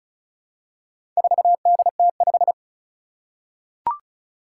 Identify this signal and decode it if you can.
Morse code, heard as a keyed tone, about 36 wpm, 700 Hz. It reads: 4DT5
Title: Four Challenging Letter-Number Combinations 35wpm